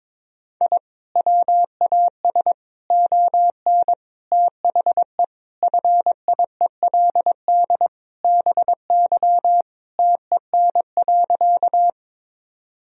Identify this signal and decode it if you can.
Morse code, heard as a keyed tone, about 22 wpm, 700 Hz. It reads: I WAS ON THE FIELD BY TEN.
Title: Sentences from Top 1000 Words - Review Part 2 - 22wpm